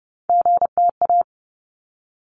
Morse code, heard as a keyed tone, about 30 wpm, 700 Hz. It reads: GTA
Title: ARRL Sections 30wpm